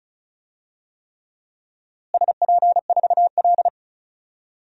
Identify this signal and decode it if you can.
Morse code, heard as a keyed tone, about 36 wpm, 700 Hz. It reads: SP4L
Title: DXCC-100 Call Signs - 4 Characters - Repeated 3x - 35wpm